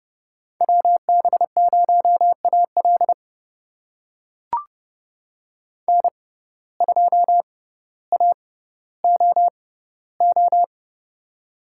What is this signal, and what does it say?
Morse code, heard as a keyed tone, about 29 wpm, spaced as for 15 wpm, 700 Hz. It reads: WB0AL N2AOO
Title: USA Call Signs - 5 Characters - 6x Character Spacing - 30wpm